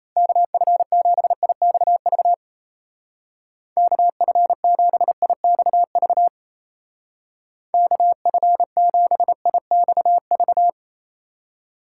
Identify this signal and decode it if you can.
Morse code, heard as a keyed tone, about 36 wpm, 700 Hz. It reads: KF7IXV KF7IXV KF7IXV